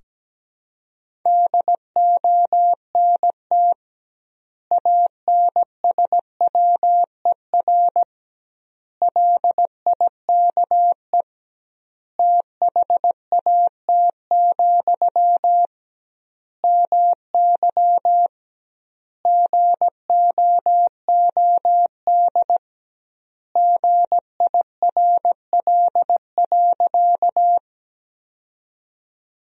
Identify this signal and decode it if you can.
Morse code, heard as a keyed tone, about 17 wpm, 700 Hz. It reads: DONT ANSWER LIKE THAT, MY GOOD GIRL.
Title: 2x Word-Spacing - Sentences from Top 400 Words - 17wpm